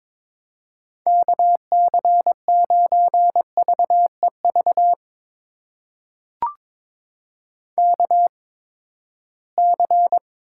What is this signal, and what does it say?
Morse code, heard as a keyed tone, about 22 wpm, spaced as for 15 wpm, 700 Hz. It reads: KC9VEV KC
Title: USA Call Signs - 6 Characters - 8x Character Spacing - 22wpm